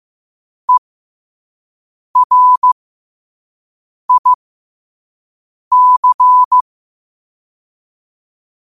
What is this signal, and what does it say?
Morse code, heard as a keyed tone, about 15 wpm, spaced as for 5 wpm, 1000 Hz. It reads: ERIC
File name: qotdfiles / 5wpm